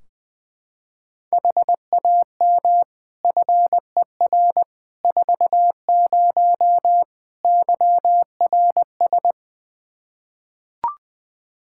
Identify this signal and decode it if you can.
Morse code, heard as a keyed tone, about 20 wpm, 700 Hz. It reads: HAM FER 40 YRS
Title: Practice QSO - Exchange 3 - More Info - 20wpm